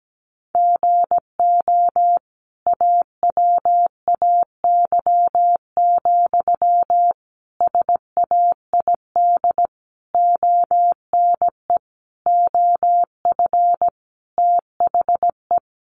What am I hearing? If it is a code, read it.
Morse code, heard as a keyed tone, about 17 wpm, 700 Hz. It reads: GO AWAY, SAID ONE OF THE
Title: Sentences from Top 1500 Words 17wpm